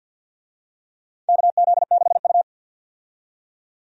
Morse code, heard as a keyed tone, about 51 wpm, 700 Hz. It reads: KZ6U
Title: USA Call Signs - 4 Characters - 50wpm